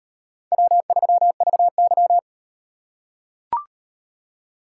Morse code, heard as a keyed tone, about 38 wpm, 700 Hz. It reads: W3VY